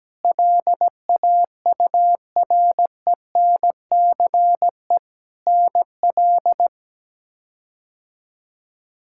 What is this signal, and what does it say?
Morse code, heard as a keyed tone, about 17 wpm, 700 Hz. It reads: LAURENCE NL
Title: K1USN SST QSO Practice - 17wpm